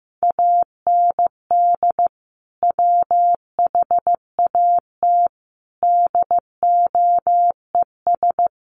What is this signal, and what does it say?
Morse code, heard as a keyed tone, about 15 wpm, 700 Hz. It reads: AND WHAT DOES
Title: Sentences from Top 1200 Words 15wpm